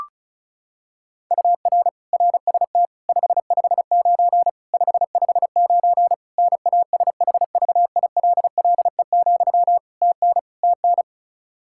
Morse code, heard as a keyed tone, about 36 wpm, 700 Hz. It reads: UR RST 559 559 NASHVILLE, TN TN
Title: Practice QSO - Exchange 1 - Beginning - 35wpm